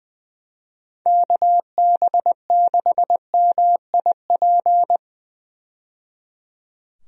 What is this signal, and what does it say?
Morse code, heard as a keyed tone, about 20 wpm, 700 Hz. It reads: KB6MIP
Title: USA Call Signs - 6 Characters - 20wpm